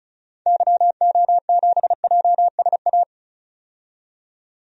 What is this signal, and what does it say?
Morse code, heard as a keyed tone, about 36 wpm, 700 Hz. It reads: YO7JSA
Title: DXCC-100 Call Signs - 6 Characters - 35wpm